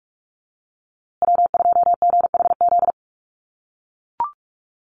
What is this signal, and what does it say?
Morse code, heard as a keyed tone, about 44 wpm, 700 Hz. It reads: W2GHZ
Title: USA Call Signs Repeated 2x 45wpm